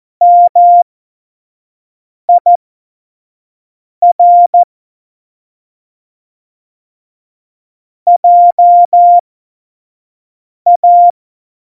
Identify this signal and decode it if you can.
Morse code, heard as a keyed tone, about 14 wpm, spaced as for 5 wpm, 700 Hz. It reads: MIR JA
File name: Morsekurs / Witze / Witz 17 April 2018, 25 F70